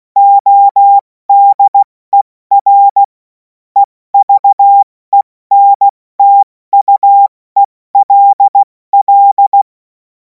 Morse code, heard as a keyed tone, about 16 wpm, 800 Hz. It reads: ODER EVENTUELL
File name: Morsekurs / Witze / Witz 21. April 2026 70 F80